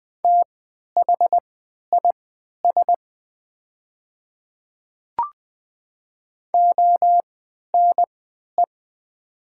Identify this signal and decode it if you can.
Morse code, heard as a keyed tone, about 20 wpm, spaced as for 11 wpm, 700 Hz. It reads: THIS ONE
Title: 3x Character Spacing - Top 100 Words - 20wpm